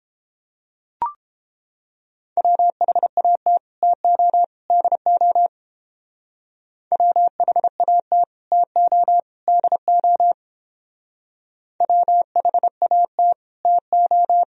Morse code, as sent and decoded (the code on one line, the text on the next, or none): .-- .... .- - / - --- / -.. --- / .-- .... .- - / - --- / -.. --- / .-- .... .- - / - ---
WHAT TO DO WHAT TO DO WHAT TO